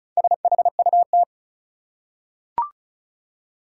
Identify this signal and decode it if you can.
Morse code, heard as a keyed tone, about 36 wpm, 700 Hz. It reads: SHUT
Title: Four Letter Words - 35wpm